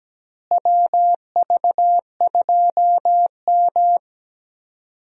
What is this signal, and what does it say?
Morse code, heard as a keyed tone, about 17 wpm, 700 Hz. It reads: WV2M
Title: USA Call Signs - 4 Characters - Repeated 2x - 17wpm